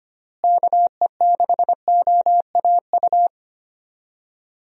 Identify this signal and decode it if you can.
Morse code, heard as a keyed tone, about 25 wpm, 700 Hz. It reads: KE6OAU
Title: USA Call Signs - 6 Characters - Repeated 3x - 25wpm